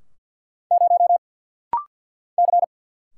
Morse code, heard as a keyed tone, about 50 wpm, 700 Hz. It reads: 0 C
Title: Single Letter-Number Alt - Rapid-Fire 50wpm